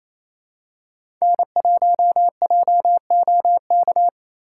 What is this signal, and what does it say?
Morse code, heard as a keyed tone, about 28 wpm, 700 Hz. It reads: N1JOK
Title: USA Call Signs - 5 Characters - 2x Character Spacing - 28wpm